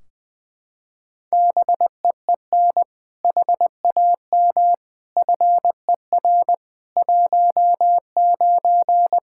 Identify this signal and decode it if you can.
Morse code, heard as a keyed tone, about 20 wpm, 700 Hz. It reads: BEEN HAM FER 19
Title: Practice QSO - Exchange 3 - More Info - 20wpm